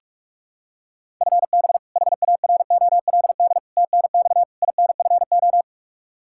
Morse code, heard as a keyed tone, about 41 wpm, 700 Hz. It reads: FB HAROLD TNX INFO
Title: Practice QSO - Exchange 3 - More Info - 45wpm